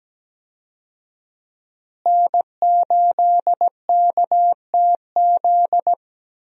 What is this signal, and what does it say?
Morse code, heard as a keyed tone, about 17 wpm, 700 Hz. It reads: N8KTZ
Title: USA Call Signs - 5 Characters - Repeated 2x - 17wpm